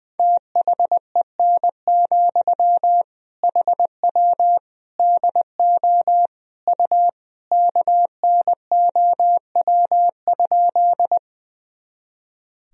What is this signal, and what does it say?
Morse code, heard as a keyed tone, about 20 wpm, 700 Hz. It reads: THEN, HW DO U KNOW?
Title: Sentences from Top 100 Words and Abbreviations 20wpm